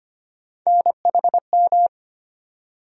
Morse code, heard as a keyed tone, about 25 wpm, 700 Hz. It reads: NHM
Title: USA Call Sign Postfixes - 3 Letters 25wpm